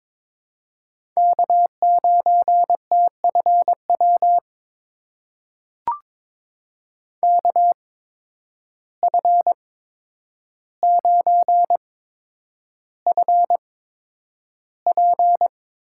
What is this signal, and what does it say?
Morse code, heard as a keyed tone, about 21 wpm, spaced as for 9 wpm, 700 Hz. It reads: K9TFW KF9FP
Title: USA Call Signs - 5 Characters - 8x Character Spacing - 22wpm